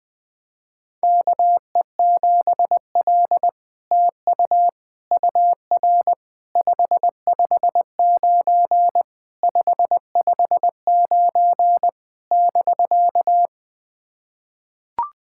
Morse code, heard as a keyed tone, about 20 wpm, 700 Hz. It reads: KE7L TU UR 559 559 <BK>
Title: SOTA QSO Practice 20wpm